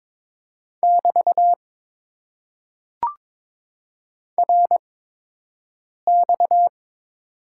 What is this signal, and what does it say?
Morse code, heard as a keyed tone, about 22 wpm, spaced as for 6 wpm, 700 Hz. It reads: <BT> RX
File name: Top 90 QSO Elements - 8x Character Spacing - 22wpm